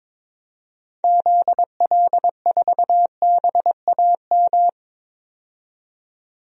Morse code, heard as a keyed tone, about 22 wpm, 700 Hz. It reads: ZL4BAM
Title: DXCC-100 Call Signs - 6 Characters - 22wpm